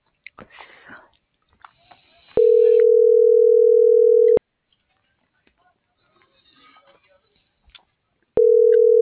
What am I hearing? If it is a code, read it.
Ringback tone (North America), 440 + 480 Hz.